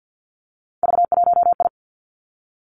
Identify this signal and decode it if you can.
Morse code, heard as a keyed tone, about 50 wpm, 700 Hz. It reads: V1I
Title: Three Challenging Letter-Number Combinations 50wpm